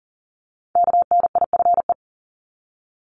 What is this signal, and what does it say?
Morse code, heard as a keyed tone, about 38 wpm, 690 Hz. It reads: KNIFE
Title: Multi-letter Phonemes 40wpm